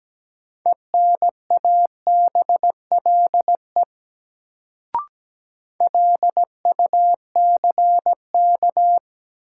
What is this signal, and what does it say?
Morse code, heard as a keyed tone, about 17 wpm, 700 Hz. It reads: ENABLE LUCK
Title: Sentences from Top 1500 Words - New Word Drill Alt - 17wpm